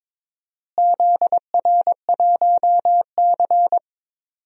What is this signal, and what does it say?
Morse code, heard as a keyed tone, about 22 wpm, 700 Hz. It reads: ZR1C
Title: DXCC-100 Call Signs - 4 Characters - Repeated 3x - 22wpm